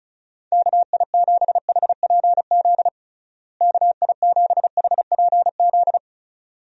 Morse code, heard as a keyed tone, about 34 wpm, 700 Hz. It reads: KI7HPZ KI7HPZ
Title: USA Call Signs - 6 Characters - Repeated 2x - 35wpm